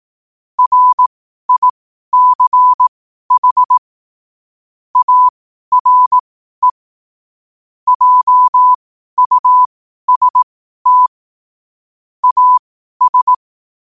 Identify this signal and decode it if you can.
Morse code, heard as a keyed tone, about 18 wpm, spaced as for 12 wpm, 1000 Hz. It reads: RICH ARE JUST AS
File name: qotdfiles / 12wpm